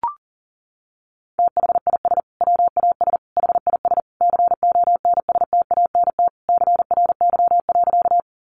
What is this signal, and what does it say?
Morse code, heard as a keyed tone, about 40 wpm, 700 Hz. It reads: THIS WAS HIS CONSTANT CRY.
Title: Sentences from Top 1500 Words 40wpm